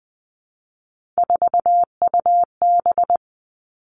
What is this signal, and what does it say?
Morse code, heard as a keyed tone, about 19 wpm, 700 Hz. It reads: 4UB
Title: Three Challenging Letter-Number Combinations 20wpm